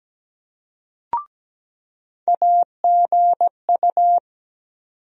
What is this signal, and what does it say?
Morse code, heard as a keyed tone, about 17 wpm, 700 Hz. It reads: AGU
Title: USA Call Sign Postfixes - 3 Letters 17wpm